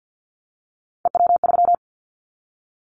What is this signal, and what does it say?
Morse code, heard as a keyed tone, about 50 wpm, 700 Hz. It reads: EW3
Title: Callsign Prefixes for Top 100 DXCC - 50wpm